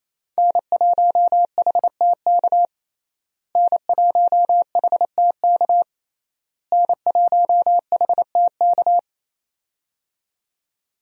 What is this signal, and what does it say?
Morse code, heard as a keyed tone, about 28 wpm, 700 Hz. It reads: N1HTK N1HTK N1HTK
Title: USA Call Signs - 5 Characters - Repeated 3x - 28wpm